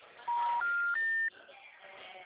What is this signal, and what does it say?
Special information tone.
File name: out- -2017- - - .2127